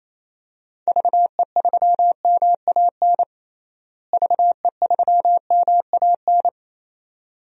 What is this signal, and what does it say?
Morse code, heard as a keyed tone, about 28 wpm, 700 Hz. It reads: VE3MAN VE3MAN
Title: DXCC-100 Call Signs - 6 Characters - Repeated 2x - 28wpm